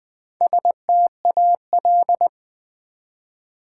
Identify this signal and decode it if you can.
Morse code, heard as a keyed tone, about 20 wpm, 700 Hz. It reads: STAL
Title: Build A Word - 1 Letter At A Time - 3 to 5-Letter Words - 20wpm